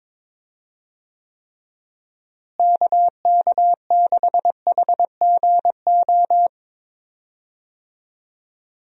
Morse code, heard as a keyed tone, about 21 wpm, 700 Hz. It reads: KK6HGO
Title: USA Call Signs - 6 Characters - Repeated 2x - 22wpm